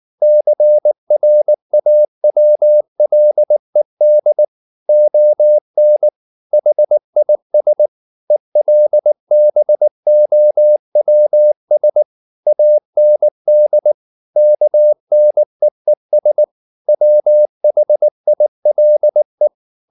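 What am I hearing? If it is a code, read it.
Morse code, heard as a keyed tone, about 19 wpm, 600 Hz. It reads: CRAWLED ON HIS ELBOWS AND KNEES WHILE
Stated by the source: Morse code at 19 wpm from CW med Gnister, reading Ken Follett's novel Never in English.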